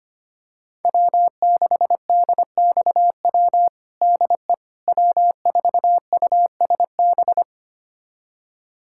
Morse code, heard as a keyed tone, about 25 wpm, 700 Hz. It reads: W6DXW DE W4USB
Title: USA Callsign DE Callsign - 5 Characters - 25wpm